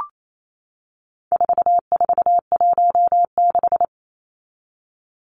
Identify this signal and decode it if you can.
Morse code, heard as a keyed tone, about 28 wpm, 700 Hz. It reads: 4416
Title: Numbers - 4-Digit 28wpm